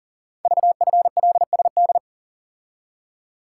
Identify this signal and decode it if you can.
Morse code, heard as a keyed tone, about 43 wpm, 700 Hz. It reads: VFLSD
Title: Code Groups - 5 Letters - 40wpm